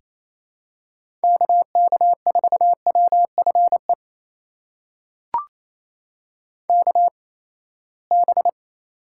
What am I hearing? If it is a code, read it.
Morse code, heard as a keyed tone, about 28 wpm, spaced as for 19 wpm, 700 Hz. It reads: KK4WFE KB
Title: USA Call Signs - 6 Characters - 8x Character Spacing - 28wpm